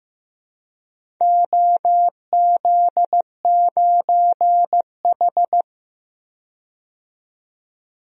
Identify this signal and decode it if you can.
Morse code, heard as a keyed tone, about 16 wpm, 700 Hz. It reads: OZ9H
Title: DXCC-100 Call Signs - 4 Characters - 15wpm